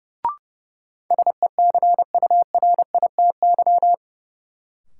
Morse code, heard as a keyed tone, about 30 wpm, 700 Hz. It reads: SECURITY
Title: Sentences from Top 1000 Words - Review Part 4 - New Word Drill - 30wpm